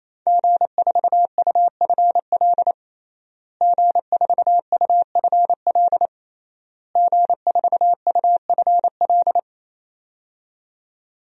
Morse code, heard as a keyed tone, about 28 wpm, 700 Hz. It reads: G4UFL G4UFL G4UFL
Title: DXCC-100 Call Signs - 5 Characters - Repeated 3x - 28wpm